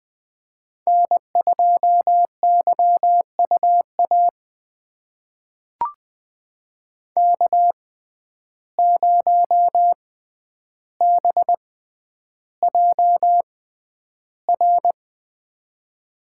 Morse code, heard as a keyed tone, about 20 wpm, spaced as for 10 wpm, 700 Hz. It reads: N2YUA K0BJR
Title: USA Call Signs - 5 Characters - 6x Character Spacing - 20wpm